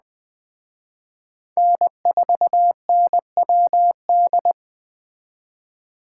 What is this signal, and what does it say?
Morse code, heard as a keyed tone, about 20 wpm, 700 Hz. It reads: N4NWD
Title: Speed Racing - USA Call Signs 17wpm